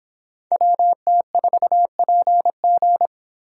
Morse code, heard as a keyed tone, about 23 wpm, 700 Hz. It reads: WT4PG